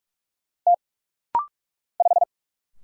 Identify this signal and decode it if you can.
Morse code, heard as a keyed tone, about 44 wpm, 700 Hz. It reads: T 5
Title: Single Letter-Number - Rapid-Fire 45wpm